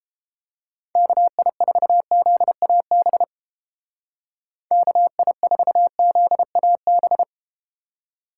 Morse code, heard as a keyed tone, about 30 wpm, 700 Hz. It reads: KI4ZAB KI4ZAB